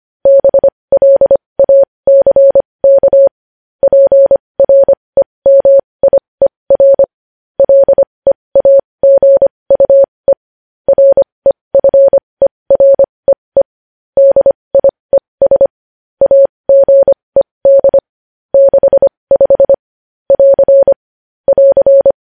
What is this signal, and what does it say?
Morse code, heard as a keyed tone, about 25 wpm, spaced as for 20 wpm, 550 Hz. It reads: BLACK PREMIER LEAGUE REFEREE DIES AGED 65 <AR> <AR>
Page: News Headlines in Morse Code at 20 WPM